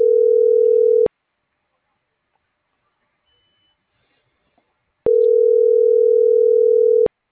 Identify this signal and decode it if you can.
Ringback tone (North America), 440 + 480 Hz.